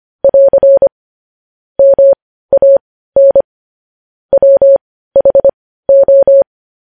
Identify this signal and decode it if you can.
Morse code, heard as a keyed tone, about 25 wpm, spaced as for 15 wpm, 550 Hz. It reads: <AR> MAN WHO
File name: podcast / News headlines-1x-25x15